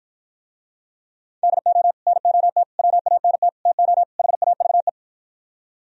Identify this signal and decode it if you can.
Morse code, heard as a keyed tone, about 53 wpm, 700 Hz. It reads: DO NOT WANT TO HAVE